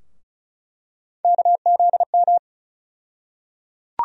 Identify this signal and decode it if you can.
Morse code, heard as a keyed tone, about 36 wpm, 700 Hz. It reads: KZM
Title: USA Call Sign Postfixes - 3 Letters 35wpm